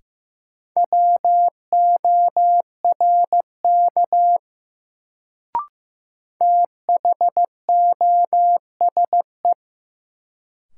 Morse code, heard as a keyed tone, about 15 wpm, 700 Hz. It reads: WORK THOSE
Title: Sets of 2 Words - New Word Drill - 15wpm